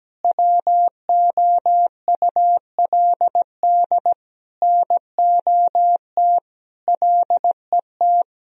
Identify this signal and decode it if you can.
Morse code, heard as a keyed tone, about 17 wpm, 700 Hz. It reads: WOULD NOT LET
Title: Sentences from Top 1000 Words - Review Part 3 - 17wpm